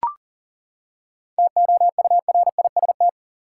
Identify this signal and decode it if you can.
Morse code, heard as a keyed tone, about 40 wpm, 700 Hz. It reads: TOURIST
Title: Seven Letter Words - 40wpm